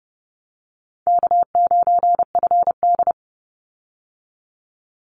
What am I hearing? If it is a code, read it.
Morse code, heard as a keyed tone, about 28 wpm, 700 Hz. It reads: K9FD
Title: Easy Call Signs - 30wpm